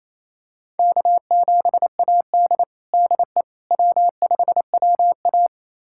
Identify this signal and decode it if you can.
Morse code, heard as a keyed tone, about 28 wpm, 700 Hz. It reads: K7AD DE W5WA
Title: USA Callsign DE Callsign - 4 Characters - 28wpm